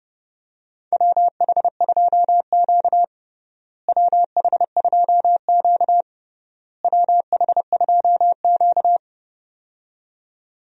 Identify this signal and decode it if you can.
Morse code, heard as a keyed tone, about 32 wpm, 700 Hz. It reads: WH2Q WH2Q WH2Q
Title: USA Call Signs - 4 Characters - Repeated 3x - 30wpm